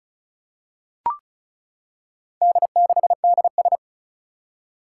Morse code, heard as a keyed tone, about 36 wpm, 700 Hz. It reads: D6DS
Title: Four Challenging Letter-Number Combinations 35wpm